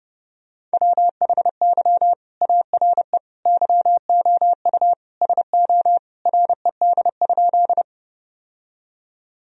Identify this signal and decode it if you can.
Morse code, heard as a keyed tone, about 30 wpm, 700 Hz. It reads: WHY ARE YOU SO RED?
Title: Sentences from Top 1000 Words - Review Part 2 - 30wpm